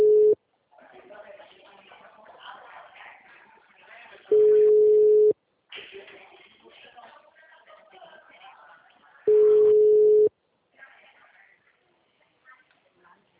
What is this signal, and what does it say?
Ringback tone (Europe, 425 Hz), with speech.